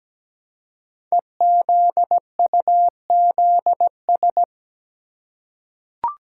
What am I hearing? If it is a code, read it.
Morse code, heard as a keyed tone, about 17 wpm, 700 Hz. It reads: EZUZS
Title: Code Groups - 5 Letters - 17wpm